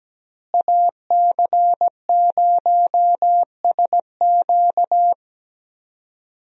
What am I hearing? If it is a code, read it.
Morse code, heard as a keyed tone, about 17 wpm, 700 Hz. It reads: AC0SQ
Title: USA Call Signs - 5 Characters - Repeated 3x - 17wpm